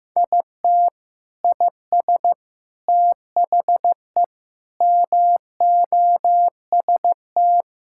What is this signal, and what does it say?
Morse code, heard as a keyed tone, about 15 wpm, 700 Hz. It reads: IT IS THE MOST
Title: Sentences from Top 1000 Words - Review Part 6 - 15wpm